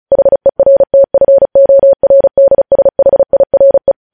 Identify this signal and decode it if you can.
Morse code, heard as a keyed tone, about 36 wpm, 550 Hz. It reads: HERTFORDSHIRE